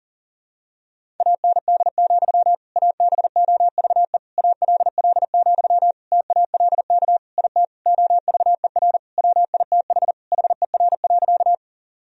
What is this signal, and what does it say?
Morse code, heard as a keyed tone, about 40 wpm, 700 Hz. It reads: AND, ABOVE ALL, TALK IT OVER WITH HER.
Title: Sentences from Top 1000 Words - Review Part 2 - 40wpm